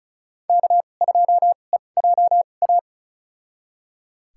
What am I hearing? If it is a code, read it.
Morse code, heard as a keyed tone, about 36 wpm, spaced as for 25 wpm, 700 Hz. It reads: K2EJA
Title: USA Call Signs - 5 Characters - 2x Character Spacing - 35wpm